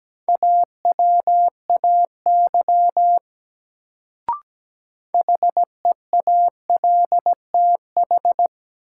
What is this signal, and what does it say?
Morse code, heard as a keyed tone, about 17 wpm, 700 Hz. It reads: AWAY HEALTH
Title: Sentences from Top 600 Words - New Word Drill - 17wpm